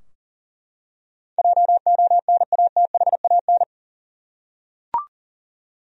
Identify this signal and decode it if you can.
Morse code, heard as a keyed tone, about 41 wpm, 700 Hz. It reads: JONATHAN
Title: USA Common Names - 40wpm